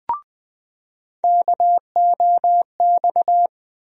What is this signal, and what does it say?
Morse code, heard as a keyed tone, about 21 wpm, 700 Hz. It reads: KOX
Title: USA Call Sign Postfixes - 3 Letters 20wpm